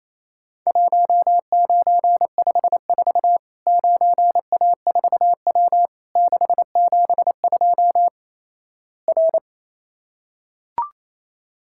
Morse code, heard as a keyed tone, about 28 wpm, 700 Hz. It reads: 1954 9A4W 672 R
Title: Worked All Europe DX Contest - QTC Exchange - 28wpm